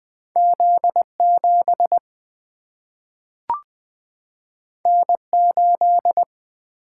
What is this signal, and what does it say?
Morse code, heard as a keyed tone, about 20 wpm, 700 Hz. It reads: Z 7 N 8
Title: Single-Letter Single-Number 20wpm